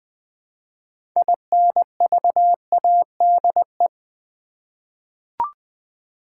Morse code, heard as a keyed tone, about 20 wpm, 700 Hz. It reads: INVADE
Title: 6 Letter Words - First 3 Letters Given - 20wpm